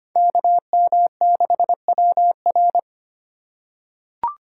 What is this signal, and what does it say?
Morse code, heard as a keyed tone, about 25 wpm, 700 Hz. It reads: KM6WR
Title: USA Call Signs - 5 Characters - 4x Character Spacing - 25wpm